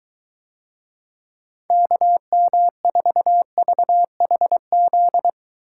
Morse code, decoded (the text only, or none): KM4VHZ